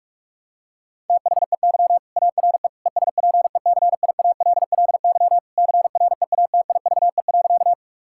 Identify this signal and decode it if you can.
Morse code, heard as a keyed tone, about 46 wpm, 700 Hz. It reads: THEY ARE ESPECIALLY CREATIVE.
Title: Sentences from Top 1000 Words - Review Part 5 - 45wpm